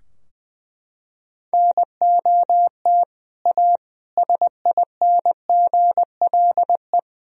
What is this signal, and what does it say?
Morse code, heard as a keyed tone, about 20 wpm, 700 Hz. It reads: NOT A SINGLE
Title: Sentences from Top 1000 Words - Review Part 4 - 20wpm